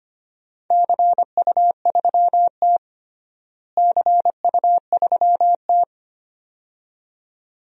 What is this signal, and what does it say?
Morse code, heard as a keyed tone, about 25 wpm, 700 Hz. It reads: CU3T CU3T
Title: DXCC-100 Call Signs - 4 Characters - Repeated 2x - 25wpm